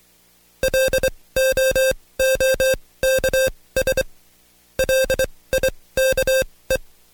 Morse code, heard as a keyed tone, about 23 wpm, spaced as for 18 wpm, 530 Hz. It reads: LOOKS LIKE
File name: Code Quick / AudioDownld2 / Seg135-Stepper8 17WPM